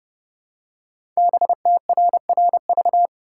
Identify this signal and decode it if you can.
Morse code, heard as a keyed tone, about 29 wpm, 700 Hz. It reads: BTRRV
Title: Code Groups - 5 Letters - 30wpm